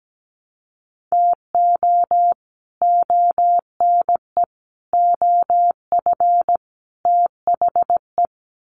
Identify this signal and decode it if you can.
Morse code, heard as a keyed tone, about 17 wpm, 700 Hz. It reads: TO ONE OF THE
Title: Sets of 4 Words 17wpm